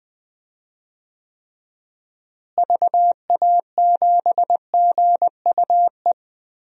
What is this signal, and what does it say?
Morse code, heard as a keyed tone, about 21 wpm, 700 Hz. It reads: VA7GUE
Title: DXCC-100 Call Signs - 6 Characters - Repeated 3x - 20wpm